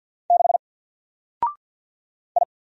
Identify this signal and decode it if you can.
Morse code, heard as a keyed tone, about 48 wpm, 700 Hz. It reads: 6 I
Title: Single Character (Letter:Number:Punctuation:Prosign) Alt 50wpm